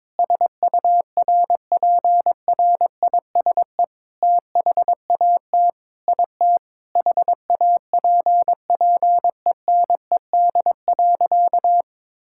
Morse code, heard as a keyed tone, about 22 wpm, 700 Hz. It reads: SURPRISE THAT IT HAPPENED.
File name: practice / 22wpm / Sentences with 8-Word Phrases - 22wpm